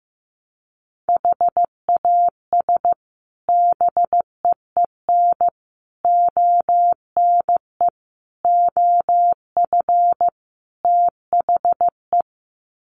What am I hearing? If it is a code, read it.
Morse code, heard as a keyed tone, about 15 wpm, 700 Hz. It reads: HAS BEEN ONE OF THE
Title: Sets of 5 Words Encore 15wpm